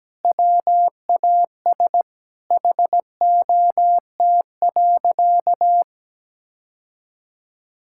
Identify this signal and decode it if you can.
Morse code, heard as a keyed tone, about 17 wpm, 700 Hz. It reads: WAS HOT.
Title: Sentences from Top 1600 Words 17wpm